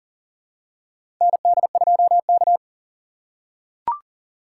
Morse code, heard as a keyed tone, about 40 wpm, 700 Hz. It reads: ND2K
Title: USA Call Signs - 4 Characters - 4x Character Spacing - 40wpm